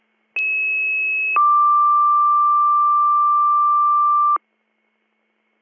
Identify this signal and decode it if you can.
Two-tone page, heard as a one-second tone, then a three-second tone, about 2600 then 1180 Hz. It is heard from 0.4 to 4.4 s.